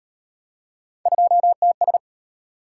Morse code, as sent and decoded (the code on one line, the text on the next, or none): ..--- - ...
2TS